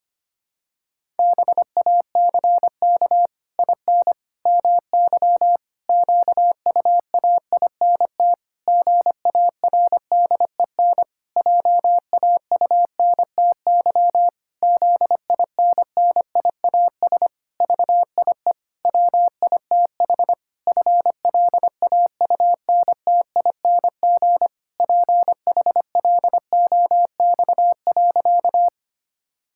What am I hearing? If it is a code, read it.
Morse code, heard as a keyed tone, about 25 wpm, 700 Hz. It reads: BACK IN MY QUAINT GARDEN JAUNTY ZINNIAS VIE WITH FLAUNTING PHLOX.
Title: Pangram Challenge v2 - 25wpm